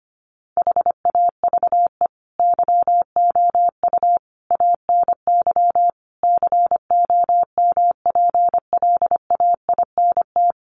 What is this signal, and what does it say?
Morse code, heard as a keyed tone, about 25 wpm, 700 Hz. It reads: HAVE YOU ANY COMPLAINT
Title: Sentences from Top 1400 Words 25wpm